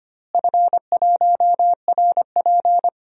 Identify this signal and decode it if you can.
Morse code, heard as a keyed tone, about 25 wpm, 700 Hz. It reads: F1RP
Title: DXCC-100 Call Signs - 4 Characters - Repeated 3x - 25wpm